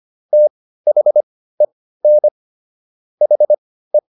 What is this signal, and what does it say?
Morse code, heard as a keyed tone, about 25 wpm, spaced as for 15 wpm, 600 Hz. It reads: THEN HE